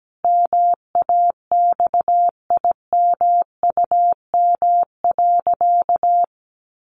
Morse code, heard as a keyed tone, about 17 wpm, 700 Hz. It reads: MAXIMUM.
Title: Sentences from Top 1300 Words 17wpm